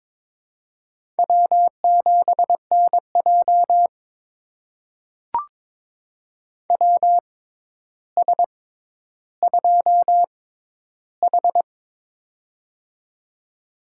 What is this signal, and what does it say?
Morse code, heard as a keyed tone, about 23 wpm, spaced as for 11 wpm, 700 Hz. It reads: W7NJ WS2H
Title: USA Call Signs - 4 Characters - 6x Character Spacing - 22wpm